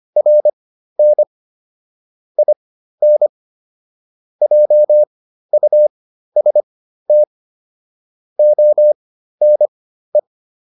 Morse code, heard as a keyed tone, about 25 wpm, spaced as for 13 wpm, 600 Hz. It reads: RN IN JUST ONE